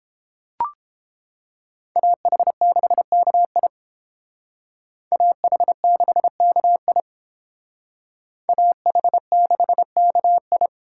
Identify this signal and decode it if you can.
Morse code, heard as a keyed tone, about 32 wpm, 700 Hz. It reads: AH6KI AH6KI AH6KI